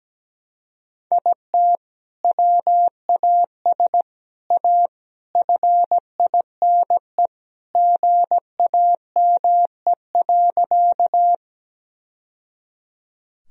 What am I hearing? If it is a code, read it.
Morse code, heard as a keyed tone, about 17 wpm, 700 Hz. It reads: IT WAS A FINE GAME.
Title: Sentences from Top 1000 Words - Review Part 3 - 17wpm